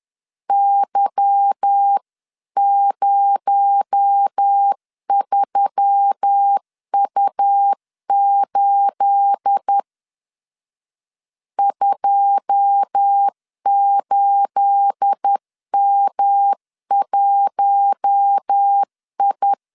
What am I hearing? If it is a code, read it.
Morse code, heard as a keyed tone, about 10 wpm, 800 Hz. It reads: Y03U8 28M1I
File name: LEZIONI / 9 28 10wpm 2842